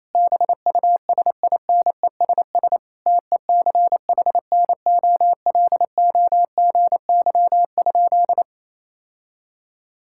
Morse code, heard as a keyed tone, about 28 wpm, 700 Hz. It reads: BUSINESS TECHNOLOGY?
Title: Sentences from Top 1000 Words - Review Part 5 - 28wpm